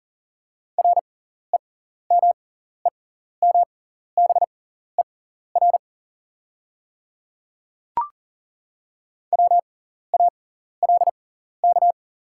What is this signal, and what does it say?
Morse code, heard as a keyed tone, about 37 wpm, spaced as for 14 wpm, 690 Hz. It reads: REMEMBER WALK
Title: 6x Character Spacing - Top 500 Words - 40wpm